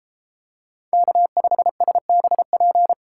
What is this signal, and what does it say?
Morse code, heard as a keyed tone, about 33 wpm, 700 Hz. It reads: K5SBP